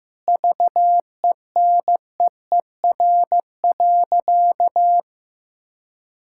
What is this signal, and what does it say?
Morse code, heard as a keyed tone, about 15 wpm, 700 Hz. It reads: VENEER.
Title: Pangram Challenge v2 - 15wpm